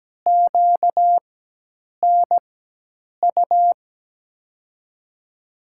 Morse code, heard as a keyed tone, about 17 wpm, spaced as for 8 wpm, 700 Hz. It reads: QNU
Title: USA Call Signs - 6 Characters - 4x Character Spacing - 17wpm